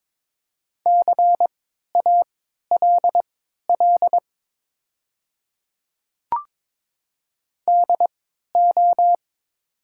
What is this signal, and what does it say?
Morse code, heard as a keyed tone, about 22 wpm, spaced as for 12 wpm, 700 Hz. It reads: CALL DO
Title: 3x Character Spacing - Top 100 Words - 22wpm